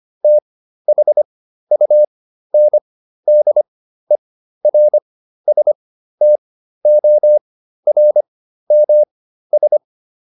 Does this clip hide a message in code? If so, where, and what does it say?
Yes: Morse code from 0.2 to 9.8 s, heard as a keyed tone, about 25 wpm, spaced as for 13 wpm, 600 Hz. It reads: THUNDERSTORMS